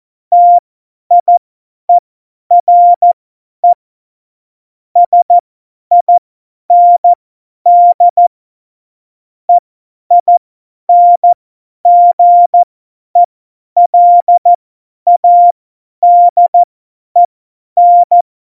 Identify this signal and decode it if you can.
Morse code, heard as a keyed tone, about 14 wpm, spaced as for 10 wpm, 700 Hz. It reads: TIERE SIND EINGELADEN
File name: Morsekurs / Witze / Witz 23. Oktober 2018, 50 F70